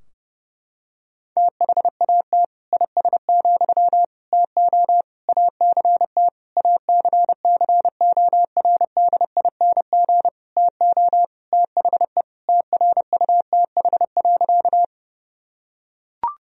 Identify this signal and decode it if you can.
Morse code, heard as a keyed tone, about 30 wpm, 700 Hz. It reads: THAT IS, TO ACT ACCORDING TO THE TRUTH.
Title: Sentences from Top 1000 Words - Review Part 4 - 30wpm